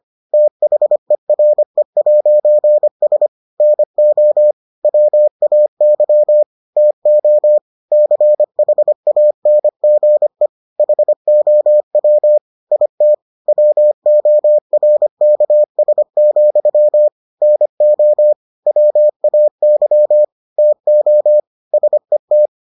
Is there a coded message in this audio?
Yes: Morse code from 0.3 to 22.5 s, heard as a keyed tone, about 25 wpm, 600 Hz. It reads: THERE'S NO WAY TO CHANGE HOW IT WORKS, NO WAY TO SET